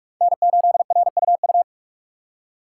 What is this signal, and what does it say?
Morse code, heard as a keyed tone, about 42 wpm, 700 Hz. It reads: N8RUU
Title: USA Call Signs - 6x Character Spacing - 45wpm